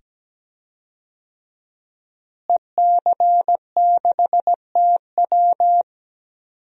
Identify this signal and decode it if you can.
Morse code, heard as a keyed tone, about 17 wpm, 700 Hz. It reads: EC6TW
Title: DXCC-100 Call Signs - 5 Characters - Repeated 2x - 17wpm